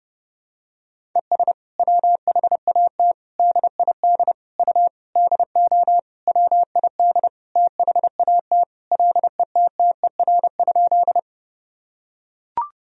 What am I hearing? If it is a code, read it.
Morse code, heard as a keyed tone, about 32 wpm, 700 Hz. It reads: ES WHAT DID U DO WID THAT LETTER?
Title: Sentences from Top 300 Words and Abbreviations 30wpm